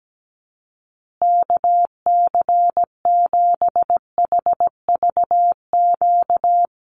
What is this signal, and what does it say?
Morse code, heard as a keyed tone, about 17 wpm, 700 Hz. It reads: KC7HVQ
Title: USA Call Signs Repeated 3x 17wpm